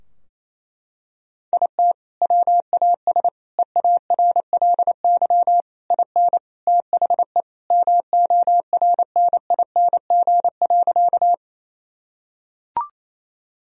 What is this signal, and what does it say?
Morse code, heard as a keyed tone, about 29 wpm, 700 Hz. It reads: IT WAS EARLY IN THE MORNING.
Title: Sentences from Top 1000 Words - Review Part 2 - 28wpm